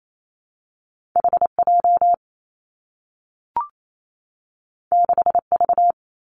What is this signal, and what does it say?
Morse code, heard as a keyed tone, about 28 wpm, 700 Hz. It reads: H J 6 V
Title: Two Challenging Letter-Number Combinations 28wpm